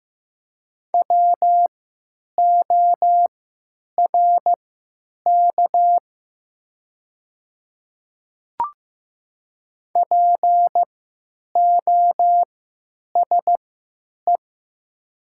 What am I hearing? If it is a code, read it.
Morse code, heard as a keyed tone, about 15 wpm, spaced as for 9 wpm, 700 Hz. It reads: WORK POSE
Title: 3x Character Spacing - Top 500 Words - 15wpm